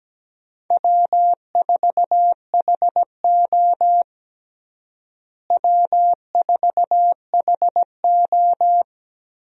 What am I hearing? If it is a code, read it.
Morse code, heard as a keyed tone, about 17 wpm, 700 Hz. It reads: W4HO W4HO
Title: USA Call Signs - 4 Characters - Repeated 2x - 17wpm